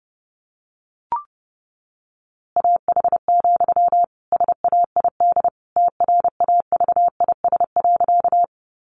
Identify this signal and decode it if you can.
Morse code, heard as a keyed tone, about 30 wpm, 700 Hz. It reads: AH, SAID TRAVIS.
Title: A Sound Of Thunder 30wpm